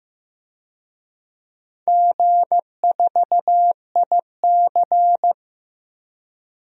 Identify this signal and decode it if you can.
Morse code, heard as a keyed tone, about 15 wpm, 700 Hz. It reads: G4IC
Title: DXCC-100 Call Signs - 4 Characters - Repeated 3x - 15wpm